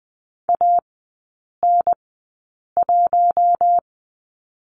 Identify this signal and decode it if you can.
Morse code, heard as a keyed tone, about 20 wpm, spaced as for 8 wpm, 700 Hz. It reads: AN1
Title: 053 Lesson 10e - Letter r - Three Characters 20wpm